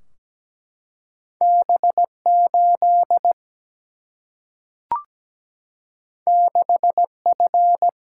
Morse code, heard as a keyed tone, about 17 wpm, 700 Hz. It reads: B 8 6 F
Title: Single-Letter Single-Number 17wpm